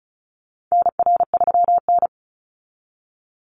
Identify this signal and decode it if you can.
Morse code, heard as a keyed tone, about 34 wpm, 700 Hz. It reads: NR3N
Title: Call Signs - 35wpm